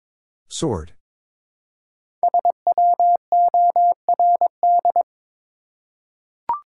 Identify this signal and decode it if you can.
Morse code, heard as a keyed tone, about 23 wpm, 700 Hz. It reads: SWORD
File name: 5 Letter Words - First 3 Letters Given - 22wpm